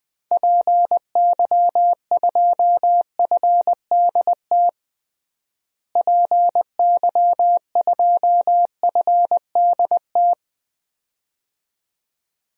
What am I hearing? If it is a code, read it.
Morse code, heard as a keyed tone, about 21 wpm, 700 Hz. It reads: PY2FDT PY2FDT
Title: DXCC-100 Call Signs - 6 Characters - Repeated 2x - 20wpm